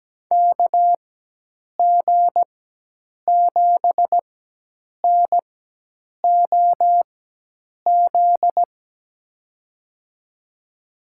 Morse code, heard as a keyed tone, about 17 wpm, spaced as for 8 wpm, 700 Hz. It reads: KG7NOZ
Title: USA Call Signs - 6 Characters - 4x Character Spacing - 17wpm